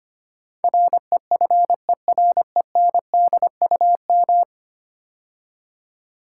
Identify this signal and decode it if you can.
Morse code, heard as a keyed tone, about 25 wpm, 700 Hz. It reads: REFERENDUM
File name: 10 Letter Words - First 3 Letters Given - 25wpm